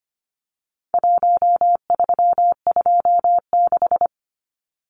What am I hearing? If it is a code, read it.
Morse code, heard as a keyed tone, about 25 wpm, 700 Hz. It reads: 1326
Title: Numbers - Contest Style - 4-Digit 25wpm